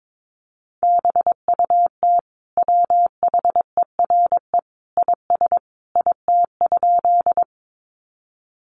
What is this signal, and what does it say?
Morse code, heard as a keyed tone, about 22 wpm, 700 Hz. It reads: BUT WHERE IS IT?
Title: Sentences from Top 200 Words 22wpm